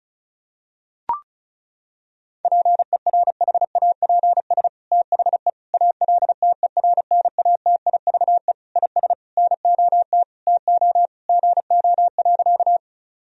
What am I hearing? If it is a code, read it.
Morse code, heard as a keyed tone, about 36 wpm, 700 Hz. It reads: PERHAPS THE ALTERNATIVE IS NOT TO GO.
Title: Sentences from Top 1000 Words - Review Part 4 - 35wpm